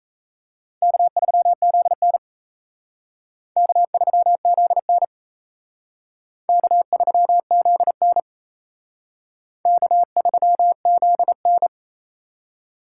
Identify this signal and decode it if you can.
Morse code, heard as a keyed tone, about 34 wpm, 700 Hz. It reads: K3ZN K3ZN K3ZN K3ZN
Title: Speed Racing - USA 1x2 Call Signs 28wpm